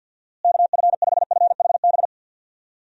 Morse code, heard as a keyed tone, about 51 wpm, 700 Hz. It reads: KF5FHB
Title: USA Call Signs - 6 Characters - Repeated 3x - 50wpm